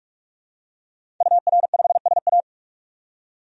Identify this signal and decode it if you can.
Morse code, heard as a keyed tone, about 46 wpm, 700 Hz. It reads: UR5SA
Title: DXCC-100 Call Signs - 5 Characters - 45wpm